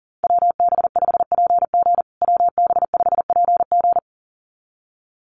Morse code, heard as a keyed tone, about 37 wpm, 700 Hz. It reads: WB5PG WB5PG
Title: USA Call Signs Repeated 2x 40wpm